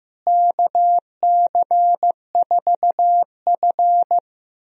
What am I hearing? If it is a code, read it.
Morse code, heard as a keyed tone, about 15 wpm, 700 Hz. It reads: KC4F
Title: USA Call Signs - 4 Characters - Repeated 2x - 15wpm